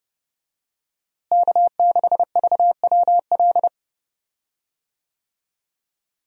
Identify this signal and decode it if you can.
Morse code, heard as a keyed tone, about 32 wpm, 700 Hz. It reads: K6VWL
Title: USA Call Signs - 5 Characters - Repeated 2x - 30wpm